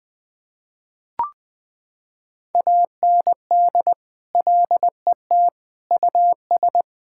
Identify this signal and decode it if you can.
Morse code, heard as a keyed tone, about 20 wpm, 700 Hz. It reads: AND LET US
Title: Sentences with 7-Word Phrases - 20wpm